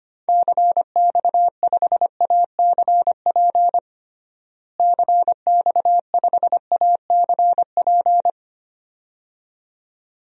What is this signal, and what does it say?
Morse code, heard as a keyed tone, about 25 wpm, 700 Hz. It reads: CX5ACP CX5ACP
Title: DXCC-100 Call Signs - 6 Characters - Repeated 2x - 25wpm